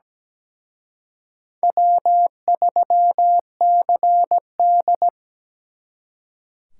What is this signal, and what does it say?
Morse code, heard as a keyed tone, about 20 wpm, 700 Hz. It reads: W3CD
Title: Speed Racing - USA Call Signs 17wpm